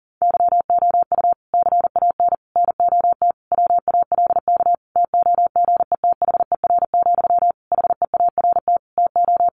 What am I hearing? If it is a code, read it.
Morse code, heard as a keyed tone, about 40 wpm, 700 Hz. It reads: YOU CAN NOT WALK TOGETHER, HEART TO
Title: Sentences from Top 1500 Words 40wpm